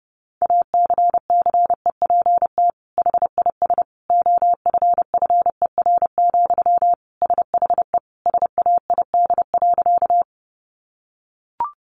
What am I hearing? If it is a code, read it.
Morse code, heard as a keyed tone, about 30 wpm, 700 Hz. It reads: ACCEPT HIS OFFER, SHE SAID.
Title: Sentences from Top 1300 Words 30wpm